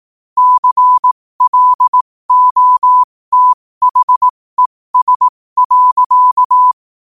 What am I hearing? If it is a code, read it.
Morse code, heard as a keyed tone, about 18 wpm, spaced as for 15 wpm, 1000 Hz. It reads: CLOTHES.
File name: 15wpm